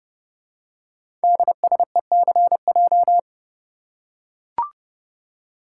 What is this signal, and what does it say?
Morse code, heard as a keyed tone, about 30 wpm, 700 Hz. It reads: DSECJ
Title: Code Groups - 5 Letters - 30wpm